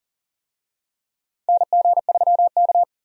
Morse code, heard as a keyed tone, about 40 wpm, 700 Hz. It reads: NG3K
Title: USA Call Signs - 4 Characters - 40wpm